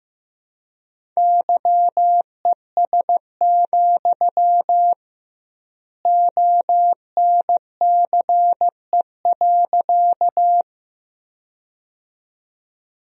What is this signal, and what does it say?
Morse code, heard as a keyed tone, about 16 wpm, 700 Hz. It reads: YES, ONCE.
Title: 2x Word-Spacing - Sentences from Top 500 Words - 15wpm